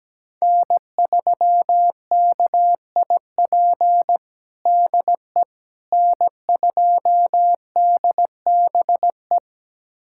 Morse code, heard as a keyed tone, about 17 wpm, 700 Hz. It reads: N3KIP DE N2DBE
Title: USA Callsign DE Callsign - 5 Characters - 17wpm